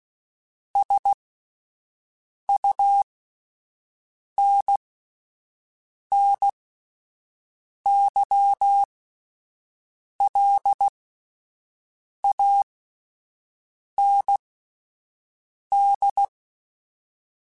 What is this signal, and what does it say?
Morse code, heard as a keyed tone, about 16 wpm, spaced as for 5 wpm, 800 Hz. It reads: SUNNYLAND